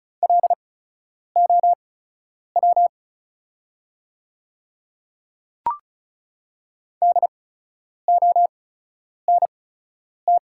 Morse code, heard as a keyed tone, about 36 wpm, spaced as for 10 wpm, 700 Hz. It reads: LOW DONT
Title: 8x Character Spacing - Top 300 Words - 35wpm